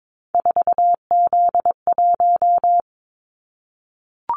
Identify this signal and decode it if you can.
Morse code, heard as a keyed tone, about 22 wpm, 700 Hz. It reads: 4Z1
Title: Callsign Prefixes for Top 50 DXCC - 22wpm